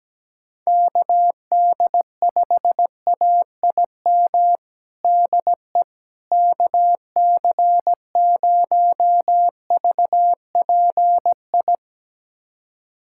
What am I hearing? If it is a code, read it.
Morse code, heard as a keyed tone, about 17 wpm, 700 Hz. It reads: KD5AIM DE KC0VPI
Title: USA Callsign DE Callsign 17wpm